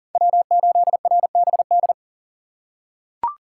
Morse code, heard as a keyed tone, about 40 wpm, 700 Hz. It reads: W8RBD
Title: USA Call Signs - 3x Character Spacing - 40wpm